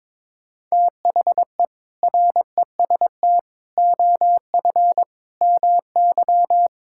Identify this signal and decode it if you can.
Morse code, heard as a keyed tone, about 22 wpm, 700 Hz. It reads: THE REST OF MY
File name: practice / 22wpm / Build A Phrase - 5-Word Phrases - 22wpm